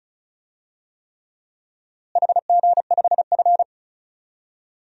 Morse code, heard as a keyed tone, about 36 wpm, 700 Hz. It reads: HG5F
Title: DXCC-100 Call Signs - 4 Characters - Repeated 3x - 35wpm